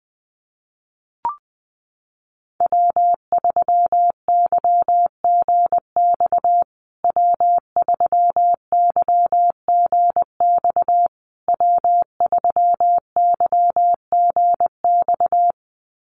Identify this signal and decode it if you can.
Morse code, heard as a keyed tone, about 20 wpm, 700 Hz. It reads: W3YGX W3YGX W3YGX
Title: USA Call Signs Repeated 3x 20wpm